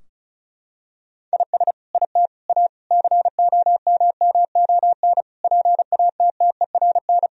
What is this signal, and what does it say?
Morse code, heard as a keyed tone, about 34 wpm, 700 Hz. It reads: IS IT A COMMON PATTERN
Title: Sentences from Top 1000 Words - Review Part 2 - 35wpm